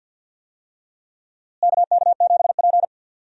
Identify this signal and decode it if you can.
Morse code, heard as a keyed tone, about 45 wpm, 700 Hz. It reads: KK7P
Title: USA Call Signs - 4 Characters - Repeated 2x - 50wpm